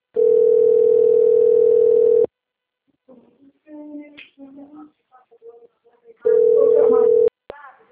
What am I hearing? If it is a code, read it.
Ringback tone (North America), 440 + 480 Hz.
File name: rg-1600-unknown- - - .28289